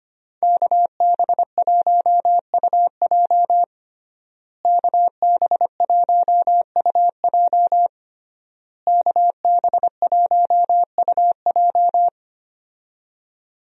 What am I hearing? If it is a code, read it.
Morse code, heard as a keyed tone, about 25 wpm, 700 Hz. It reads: KB1UJ KB1UJ KB1UJ
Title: USA Call Signs - 5 Characters - Repeated 3x - 25wpm